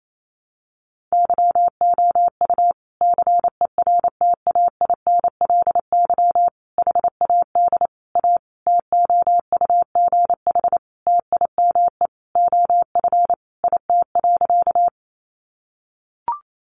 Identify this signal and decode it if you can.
Morse code, heard as a keyed tone, about 28 wpm, 700 Hz. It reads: YOU CERTAINLY HAD A TOUGH TIME OF IT.
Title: Sentences from Top 1200 Words 28wpm